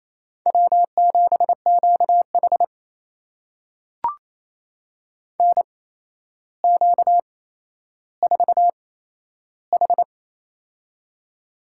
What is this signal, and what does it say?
Morse code, heard as a keyed tone, about 28 wpm, spaced as for 12 wpm, 700 Hz. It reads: W7QH NQ4H
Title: USA Call Signs - 8x Character Spacing - 28wpm